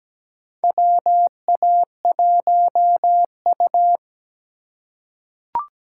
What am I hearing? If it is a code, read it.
Morse code, heard as a keyed tone, about 17 wpm, 700 Hz. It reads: WA1U
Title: USA Call Signs - 4 Characters - 2x Character Spacing - 17wpm